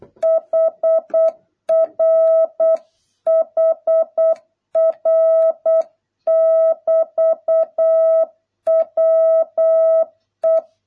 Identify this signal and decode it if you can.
Morse code, heard as a keyed tone, about 8 wpm, 650 Hz. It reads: HR HR <BT> WE